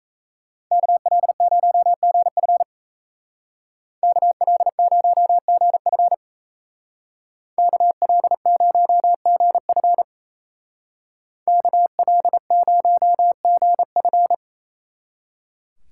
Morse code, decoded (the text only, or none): KL0GF KL0GF KL0GF KL0GF